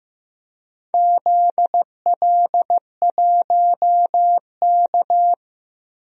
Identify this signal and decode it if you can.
Morse code, heard as a keyed tone, about 15 wpm, 700 Hz. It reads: ZL1K
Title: DXCC-100 Call Signs - 4 Characters - 15wpm